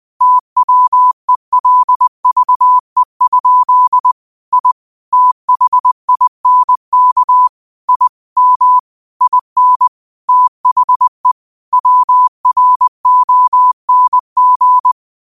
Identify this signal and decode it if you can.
Morse code, heard as a keyed tone, about 20 wpm, 1000 Hz. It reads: TWELVE? I THINK I M IN THE WRONG